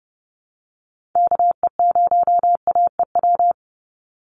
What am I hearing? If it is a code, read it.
Morse code, heard as a keyed tone, about 30 wpm, 700 Hz. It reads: KE0AEW
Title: USA Call Signs 30wpm